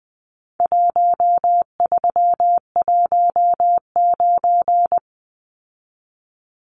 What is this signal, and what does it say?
Morse code, heard as a keyed tone, about 20 wpm, 700 Hz. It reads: 1319
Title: Numbers - Contest Style - 4-Digit 20wpm